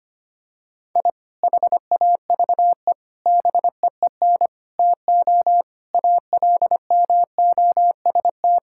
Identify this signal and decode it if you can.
Morse code, heard as a keyed tone, about 25 wpm, 700 Hz. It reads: I HAVE BEEN TO ALMOST
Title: Sentences from Top 1000 Words - Review Part 5 - 25wpm